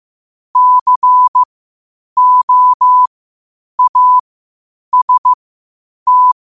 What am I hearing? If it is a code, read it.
Morse code, heard as a keyed tone, about 15 wpm, spaced as for 8 wpm, 1000 Hz. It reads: COAST
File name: qotdfiles / 8wpm